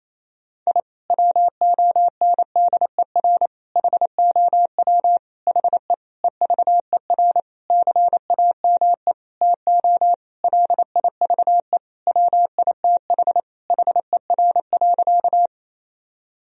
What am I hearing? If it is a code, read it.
Morse code, heard as a keyed tone, about 28 wpm, 700 Hz. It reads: I WONDER HOW HE EVER CAME TO LIVE WITH HER.
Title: Sentences from Top 1000 Words - Review Part 3 - 28wpm